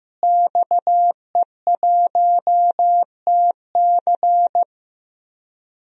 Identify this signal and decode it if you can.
Morse code, heard as a keyed tone, about 15 wpm, 700 Hz. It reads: XE1TC
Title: DXCC-100 Call Signs - 5 Characters - Repeated 3x - 15wpm